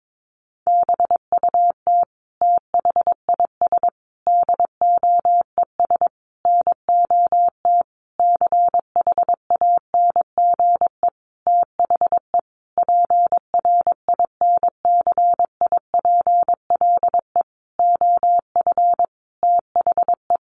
Morse code, heard as a keyed tone, about 22 wpm, 700 Hz. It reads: BUT THIS DOES NOT CHANGE THE PRINCIPLE OF THE
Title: Sentences from Top 1400 Words 22wpm